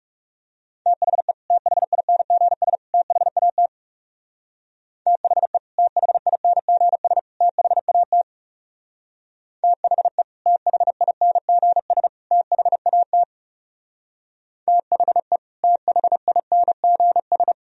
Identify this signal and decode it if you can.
Morse code, heard as a keyed tone, about 38 wpm, 700 Hz. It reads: THE THINGS THAT THE THINGS THAT THE THINGS THAT THE THINGS
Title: Speed-Racing - Sets of 3 Words 30wpm